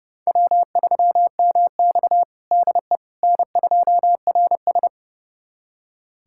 Morse code, heard as a keyed tone, about 30 wpm, 700 Hz. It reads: W3MX DE N2RS
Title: USA Callsign DE Callsign - 4 Characters - 30wpm